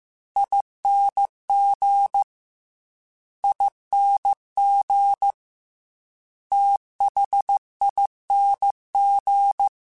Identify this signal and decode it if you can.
Morse code, heard as a keyed tone, about 15 wpm, 800 Hz. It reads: ING ING THING
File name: Special INGs / Special INGs-mp3s / Slow ing @13